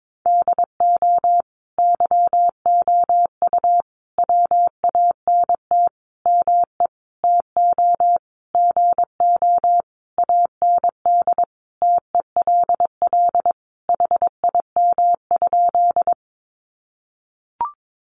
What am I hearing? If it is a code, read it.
Morse code, heard as a keyed tone, about 22 wpm, 700 Hz. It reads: DO YOU WANT ME TO GO AND TELL HIM?
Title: Sentences from Top 200 Words 22wpm